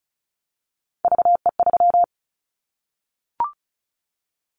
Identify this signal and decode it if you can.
Morse code, heard as a keyed tone, about 34 wpm, 700 Hz. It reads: VE3
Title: Callsign Prefixes for Top 25 DXCC - 35wpm